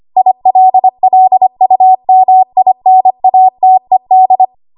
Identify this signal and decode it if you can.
Morse code, heard as a keyed tone, about 25 wpm, 750 Hz. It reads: ILLUMINATED
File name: Archive / 25 WPM / 25WPM